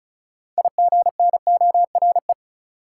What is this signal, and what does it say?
Morse code, heard as a keyed tone, about 36 wpm, 700 Hz. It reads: IGNORE
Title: Six Letter Words - 35wpm